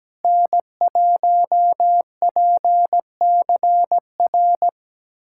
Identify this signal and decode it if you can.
Morse code, heard as a keyed tone, about 17 wpm, 700 Hz. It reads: N1PCR
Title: USA Callsign DE Callsign - 5 Characters - 17wpm